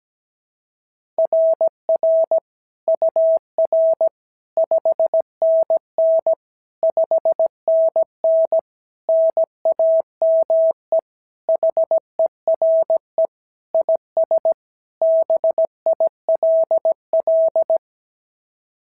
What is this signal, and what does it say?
Morse code, heard as a keyed tone, about 16 wpm, 650 Hz. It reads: RR UR 5NN 5NN NAME HERE IS BILL
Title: SOTA QSO Practice 17wpm